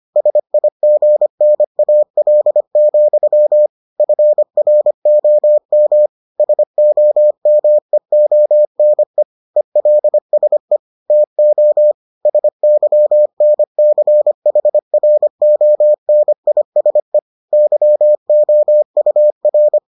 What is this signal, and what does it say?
Morse code, heard as a keyed tone, about 25 wpm, 600 Hz. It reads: SIGNAL, FROM SOMEONE ELSE TO SYNCHRONISE YOUR